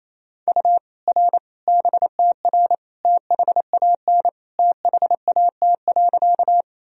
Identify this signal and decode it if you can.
Morse code, heard as a keyed tone, about 28 wpm, 700 Hz. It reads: U R BTR THAN THAT.
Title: Sentences from Top 400 Words and Abbreviations 28wpm